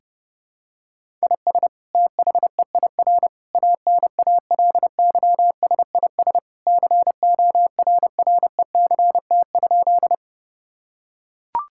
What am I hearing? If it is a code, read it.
Morse code, heard as a keyed tone, about 32 wpm, 700 Hz. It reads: IS THEIR ANALYSIS CORRECT?
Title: Sentences from Top 1000 Words - Review Part 5 - 30wpm